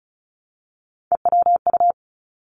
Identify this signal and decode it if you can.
Morse code, heard as a keyed tone, about 36 wpm, 700 Hz. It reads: EWU
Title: Three Challenging Letter-Number Combinations 35wpm